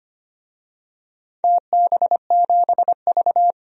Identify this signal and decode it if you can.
Morse code, heard as a keyed tone, about 25 wpm, 700 Hz. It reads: TB7V
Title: DXCC-100 Call Signs - 4 Characters - 25wpm